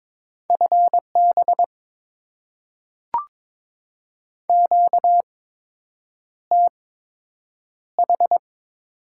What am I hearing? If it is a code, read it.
Morse code, heard as a keyed tone, about 23 wpm, spaced as for 8 wpm, 700 Hz. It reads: FB QTH
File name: Top 20 QSO Elements - 8x Character Spacing - 22wpm